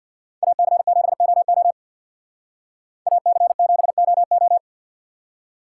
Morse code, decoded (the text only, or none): AC7OO AC7OO